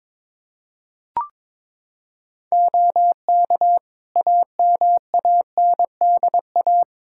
Morse code, heard as a keyed tone, about 22 wpm, 700 Hz. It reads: OK AMANDA
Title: K1USN SST QSO Practice 22wpm